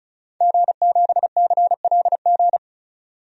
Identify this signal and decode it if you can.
Morse code, heard as a keyed tone, about 36 wpm, 700 Hz. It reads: G7CLG
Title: DXCC-100 Call Signs - 5 Characters - 35wpm